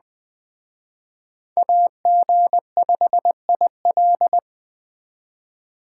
Morse code, heard as a keyed tone, about 21 wpm, 700 Hz. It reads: AG5IL